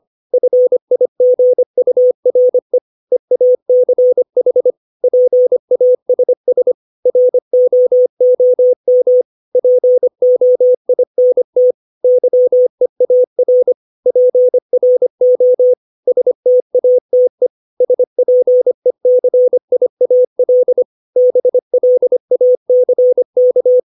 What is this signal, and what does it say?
Morse code, heard as a keyed tone, about 25 wpm, 500 Hz. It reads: FIGURE EACH PASS ROOM POINT YEAR PRO STATE SPECIAL BLACK
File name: 632 words / 632 Words 25WPM